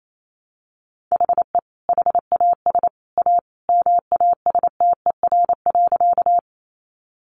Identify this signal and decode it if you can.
Morse code, heard as a keyed tone, about 28 wpm, 700 Hz. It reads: HE HAS A MASTER.
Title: Sentences from Top 1300 Words 28wpm